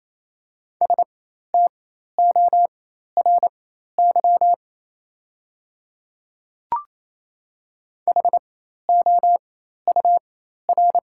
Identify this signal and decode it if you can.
Morse code, heard as a keyed tone, about 28 wpm, spaced as for 13 wpm, 690 Hz. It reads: STORY HOUR
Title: 4x Character Spacing - Top 500 Words - 28wpm